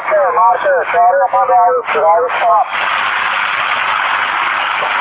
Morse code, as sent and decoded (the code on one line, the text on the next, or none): none